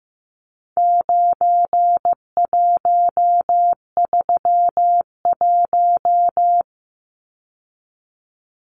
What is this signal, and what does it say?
Morse code, heard as a keyed tone, about 15 wpm, 700 Hz. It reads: 9131
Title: Numbers - 4-Digit 15wpm